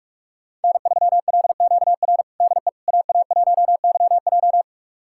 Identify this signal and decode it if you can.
Morse code, heard as a keyed tone, about 46 wpm, 700 Hz. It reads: N3LQR DE AA1YJ
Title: USA Callsign DE Callsign - 5 Characters - 45wpm